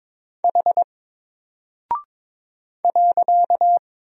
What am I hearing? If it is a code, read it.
Morse code, heard as a keyed tone, about 22 wpm, 700 Hz. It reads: H .
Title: Single Character (Letter:Number:Punctuation:Prosign) Alt 22wpm